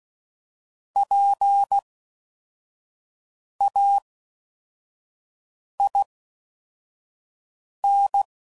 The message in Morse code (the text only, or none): PAIN